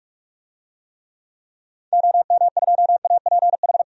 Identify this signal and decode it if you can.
Morse code, heard as a keyed tone, about 46 wpm, 700 Hz. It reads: OM2APH
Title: DXCC-100 Call Signs - 6 Characters - 45wpm